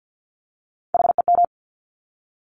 Morse code, heard as a keyed tone, about 50 wpm, 700 Hz. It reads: HEM
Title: Top Three Letter Combinations in Words 50wpm